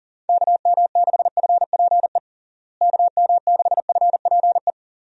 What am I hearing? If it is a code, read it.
Morse code, heard as a keyed tone, about 40 wpm, 700 Hz. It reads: KM6FPE KM6FPE
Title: USA Call Signs - 6 Characters - Repeated 2x - 40wpm